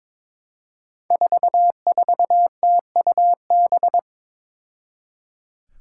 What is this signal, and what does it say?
Morse code, heard as a keyed tone, about 22 wpm, 700 Hz. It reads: 44TUB
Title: Five Challenging Letter-Number Combinations 22wpm